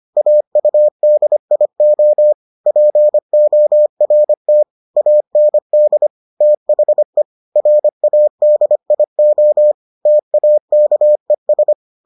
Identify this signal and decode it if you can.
Morse code, heard as a keyed tone, about 25 wpm, 600 Hz. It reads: AUDIO PORT AND THE RADIO TAKES